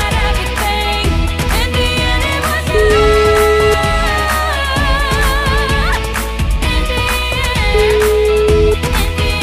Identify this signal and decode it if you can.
Ringback tone (Europe, 425 Hz), with noise.